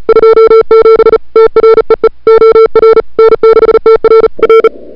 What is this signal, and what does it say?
Morse code, heard as a keyed tone, about 37 wpm, 440 Hz. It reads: 27 TREE OR N6TR R
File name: CWSprint / February 2007 CW Sprint / 01-2007 CW Sprint Hour1